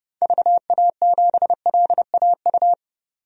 Morse code, heard as a keyed tone, about 30 wpm, 700 Hz. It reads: VA7LAU
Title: DXCC-100 Call Signs - 6 Characters - 30wpm